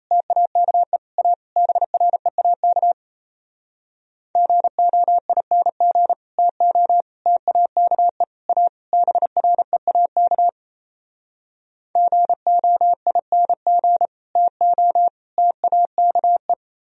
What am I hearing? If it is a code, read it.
Morse code, heard as a keyed tone, about 37 wpm, 700 Hz. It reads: TAKE A BREAK GOING TO TAKE A BREAK GOING TO TAKE